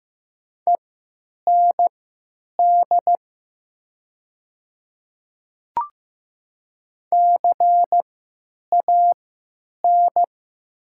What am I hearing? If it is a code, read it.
Morse code, heard as a keyed tone, about 15 wpm, spaced as for 9 wpm, 700 Hz. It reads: END CAN
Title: 3x Character Spacing - Top 200 Words - 15wpm